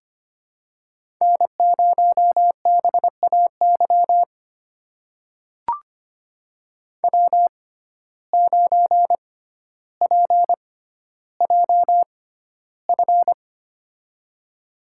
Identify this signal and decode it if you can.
Morse code, heard as a keyed tone, about 25 wpm, spaced as for 13 wpm, 700 Hz. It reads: N0BAY W9PJF
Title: USA Call Signs - 5 Characters - 6x Character Spacing - 25wpm